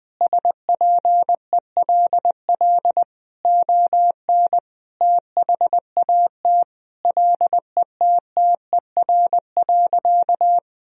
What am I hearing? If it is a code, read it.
Morse code, heard as a keyed tone, about 21 wpm, 700 Hz. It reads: SPELL ON THAT LETTER.
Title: Sentences from Top 1000 Words - Review Part 1 - 20wpm